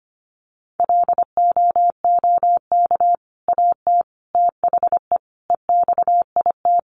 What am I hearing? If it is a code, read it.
Morse code, heard as a keyed tone, about 25 wpm, 690 Hz. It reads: LOOK AT THE EXIT
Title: Sentences from Top 1100 Words 25wpm